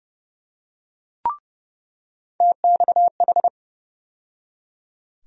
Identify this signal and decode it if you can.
Morse code, heard as a keyed tone, about 30 wpm, 700 Hz. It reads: TXH
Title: USA Call Sign Postfixes - 3 Letters 30wpm